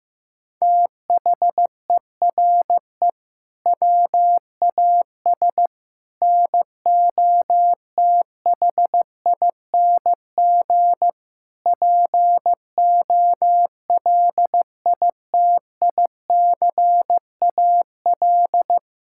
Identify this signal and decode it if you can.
Morse code, heard as a keyed tone, about 16 wpm, 700 Hz. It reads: THERE WAS NOTHING POLITICAL
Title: Sentences from Top 1000 Words - Review Part 5 - 15wpm